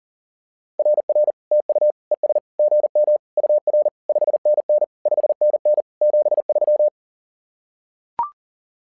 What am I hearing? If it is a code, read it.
Morse code, heard as a keyed tone, about 40 wpm, 600 Hz. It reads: RR TU ES GM UR 5NN 5NN 73
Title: SOTA QSO Practice 40wpm